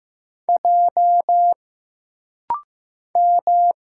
Morse code, heard as a keyed tone, about 15 wpm, 700 Hz. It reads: J M
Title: Single Letter-Number Alt - Rapid-Fire 15wpm